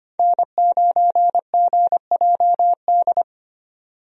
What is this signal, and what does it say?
Morse code, heard as a keyed tone, about 25 wpm, 700 Hz. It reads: N9GJD
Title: USA Call Signs - 5 Characters - 25wpm